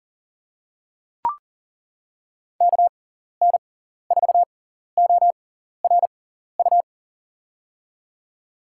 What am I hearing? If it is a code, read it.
Morse code, heard as a keyed tone, about 39 wpm, spaced as for 14 wpm, 700 Hz. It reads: KN4ORU
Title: USA Call Signs - 6 Characters - 6x Character Spacing - 40wpm